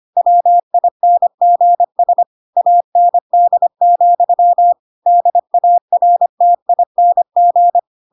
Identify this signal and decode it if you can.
Morse code, heard as a keyed tone, about 25 wpm, 700 Hz. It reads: WINGS AND, DARTING
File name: Pinocchio / Pinocchio ch5 25 25 ik1hge